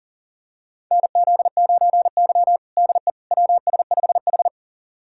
Morse code, heard as a keyed tone, about 37 wpm, 700 Hz. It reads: NZ9Y DE WS5H
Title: USA Callsign DE Callsign - 4 Characters - 40wpm